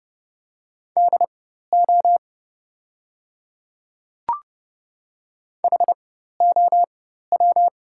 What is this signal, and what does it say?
Morse code, heard as a keyed tone, about 31 wpm, spaced as for 14 wpm, 690 Hz. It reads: DO HOW
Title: 4x Character Spacing - Top 100 Words - 30wpm